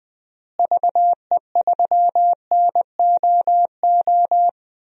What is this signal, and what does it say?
Morse code, heard as a keyed tone, about 20 wpm, 700 Hz. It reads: VE3NOO
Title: DXCC-100 Call Signs - 6 Characters - Repeated 2x - 20wpm